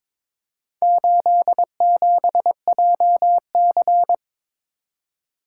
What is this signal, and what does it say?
Morse code, heard as a keyed tone, about 22 wpm, 700 Hz. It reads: 87JC
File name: practice / 22wpm / Four Letter-Number Combinations - 22wpm